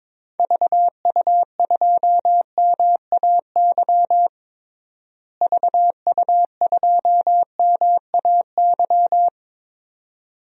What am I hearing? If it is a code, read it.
Morse code, heard as a keyed tone, about 21 wpm, 700 Hz. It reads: VU2MAY VU2MAY
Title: DXCC-100 Call Signs - 6 Characters - Repeated 2x - 22wpm